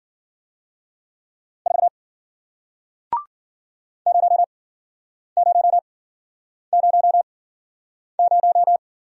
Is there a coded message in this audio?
Yes: Morse code from 1.7 to 8.8 s, heard as a keyed tone, about 54 wpm, spaced as for 9 wpm, 700 Hz. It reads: 4 0000